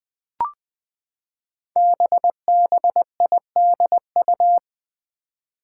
Morse code, heard as a keyed tone, about 20 wpm, 700 Hz. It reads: BBIDU
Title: Five Challenging Letter-Number Combinations 20wpm